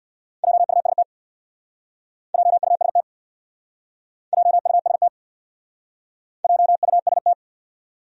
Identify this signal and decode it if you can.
Morse code, heard as a keyed tone, about 64 wpm, 700 Hz. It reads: JUST JUST JUST JUST